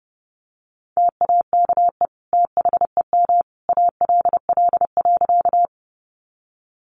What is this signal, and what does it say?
Morse code, heard as a keyed tone, about 30 wpm, 700 Hz. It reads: TAKE THEM ALL.
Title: Sentences from Top 200 Words 30wpm